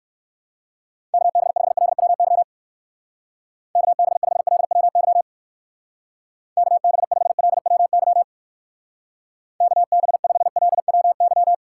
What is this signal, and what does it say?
Morse code, heard as a keyed tone, about 59 wpm, 700 Hz. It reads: KB5LWY KB5LWY KB5LWY KB5LWY